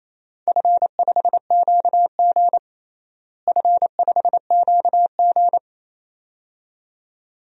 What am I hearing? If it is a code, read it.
Morse code, heard as a keyed tone, about 28 wpm, 700 Hz. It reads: F5QG F5QG
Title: DXCC-100 Call Signs - 4 Characters - Repeated 2x - 28wpm